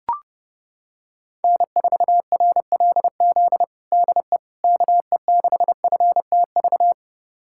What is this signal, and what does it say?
Morse code, heard as a keyed tone, about 32 wpm, 700 Hz. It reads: N4RLZ DE KE6FTV
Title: USA Callsign DE Callsign 30wpm